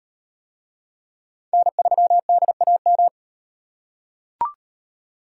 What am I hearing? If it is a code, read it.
Morse code, heard as a keyed tone, about 38 wpm, 700 Hz. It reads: N3DAM